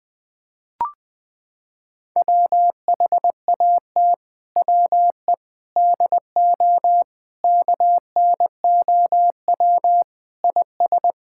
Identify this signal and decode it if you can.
Morse code, heard as a keyed tone, about 21 wpm, 700 Hz. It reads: WHAT WE DO KNOW IS
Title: Sets of 6 Words - 20wpm